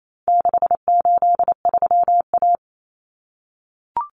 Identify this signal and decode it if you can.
Morse code, heard as a keyed tone, about 28 wpm, 700 Hz. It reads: 683A
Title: Common Cut Numbers - 4-Digit 28wpm